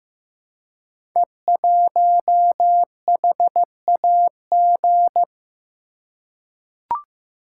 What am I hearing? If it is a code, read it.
Morse code, heard as a keyed tone, about 16 wpm, 700 Hz. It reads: E1HAG
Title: Code Groups - 5 Letter:Numbers - 15wpm